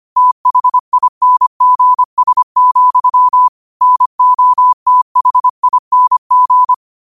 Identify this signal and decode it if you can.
Morse code, heard as a keyed tone, about 25 wpm, 1000 Hz. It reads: THINGS, NOTHING
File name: qotdfiles / 25wpm